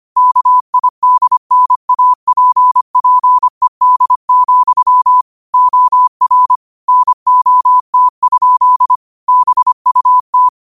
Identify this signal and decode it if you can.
Morse code, heard as a keyed tone, about 25 wpm, 1000 Hz. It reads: KIDNAPPED, OR NOT? BUT
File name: qotdfiles / 25wpm